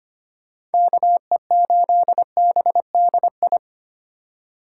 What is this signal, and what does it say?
Morse code, heard as a keyed tone, about 33 wpm, 700 Hz. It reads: KE8BDI